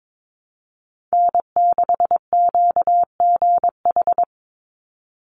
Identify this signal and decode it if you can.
Morse code, heard as a keyed tone, about 22 wpm, 700 Hz. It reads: N6QGH
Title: USA Call Signs 22wpm